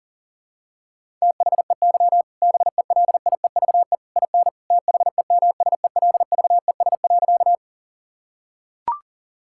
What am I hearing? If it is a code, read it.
Morse code, heard as a keyed tone, about 40 wpm, 700 Hz. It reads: THEY BELIEVE IN THEMSELVES.
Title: Sentences from Top 1000 Words - Review Part 5 - 40wpm